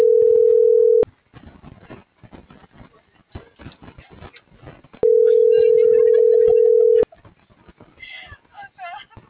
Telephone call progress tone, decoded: ringback tone

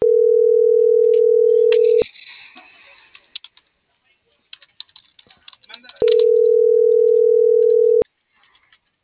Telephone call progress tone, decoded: ringback tone

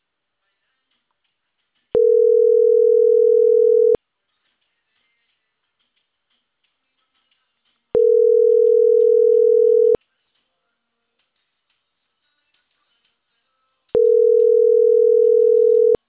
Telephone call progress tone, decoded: ringback tone

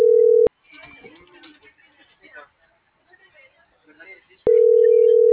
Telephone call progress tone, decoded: ringback tone

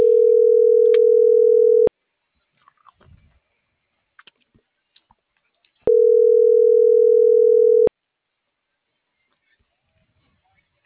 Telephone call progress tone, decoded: ringback tone

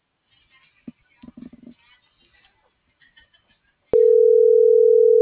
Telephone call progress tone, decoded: ringback tone

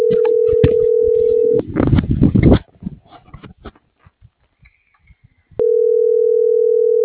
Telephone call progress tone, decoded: ringback tone

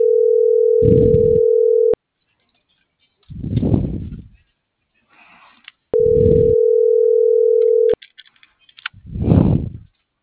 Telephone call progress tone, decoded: ringback tone

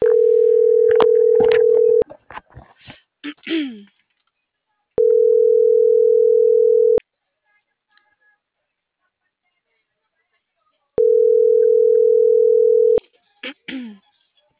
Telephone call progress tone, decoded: ringback tone

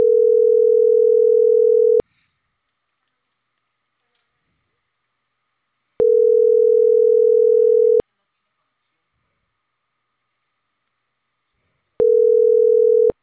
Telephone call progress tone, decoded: ringback tone

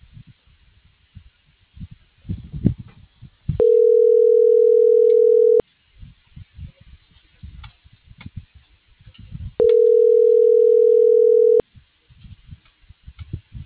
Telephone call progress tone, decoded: ringback tone

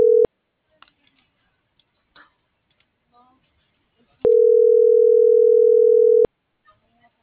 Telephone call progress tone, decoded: ringback tone

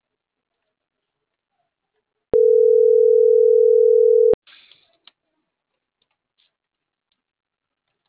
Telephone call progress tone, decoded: ringback tone

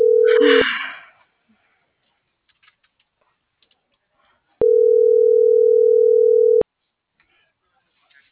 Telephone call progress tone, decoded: ringback tone

